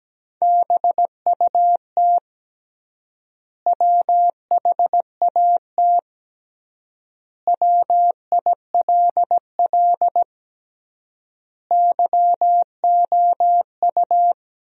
Morse code, heard as a keyed tone, about 17 wpm, 700 Hz. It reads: BUT WHAT WILL YOU
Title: 3x Word-Spacing - Sentences from Top 100 Words - 17wpm